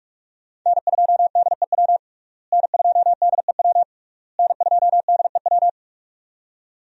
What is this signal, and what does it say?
Morse code, heard as a keyed tone, about 41 wpm, 700 Hz. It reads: N2DEW N2DEW N2DEW
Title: USA Call Signs - 5 Characters - Repeated 3x - 45wpm